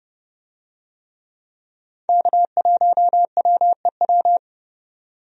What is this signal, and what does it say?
Morse code, heard as a keyed tone, about 30 wpm, 700 Hz. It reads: K1WEW
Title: USA Call Signs - 5 Characters - 30wpm